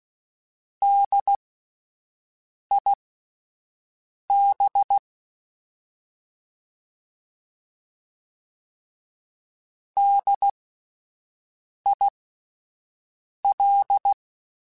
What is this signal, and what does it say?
Morse code, heard as a keyed tone, about 16 wpm, spaced as for 5 wpm, 800 Hz. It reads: DIB DIL